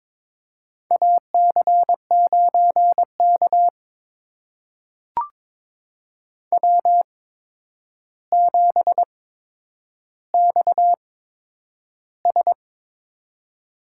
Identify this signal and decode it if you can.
Morse code, heard as a keyed tone, about 22 wpm, spaced as for 9 wpm, 700 Hz. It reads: AC9K W7XS
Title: USA Call Signs - 4 Characters - 8x Character Spacing - 22wpm